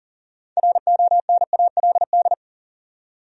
Morse code, heard as a keyed tone, about 38 wpm, 700 Hz. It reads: RONALD
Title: USA Common Names - 40wpm